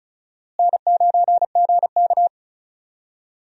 Morse code, heard as a keyed tone, about 34 wpm, 700 Hz. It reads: N9GK
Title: USA Call Signs - 4 Characters - Repeated 3x - 35wpm